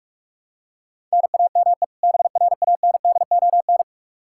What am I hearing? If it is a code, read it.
Morse code, heard as a keyed tone, about 43 wpm, 700 Hz. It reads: NAME BRANDON
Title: K1USN SST QSO Practice 45wpm